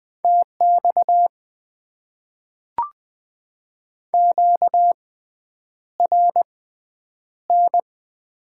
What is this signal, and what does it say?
Morse code, heard as a keyed tone, about 20 wpm, spaced as for 9 wpm, 700 Hz. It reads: TX QRN
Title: Top 90 QSO Elements - 6x Character Spacing - 20wpm